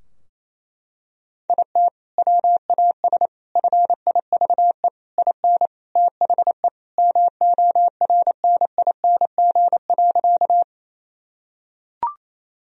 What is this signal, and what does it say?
Morse code, heard as a keyed tone, about 28 wpm, 700 Hz. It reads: IT WAS FIVE IN THE MORNING.
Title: Sentences from Top 1000 Words - Review Part 2 - 28wpm